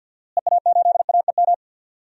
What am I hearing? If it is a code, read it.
Morse code, heard as a keyed tone, about 46 wpm, 700 Hz. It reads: EA8AEM
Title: DXCC-100 Call Signs - 6 Characters - Repeated 3x - 50wpm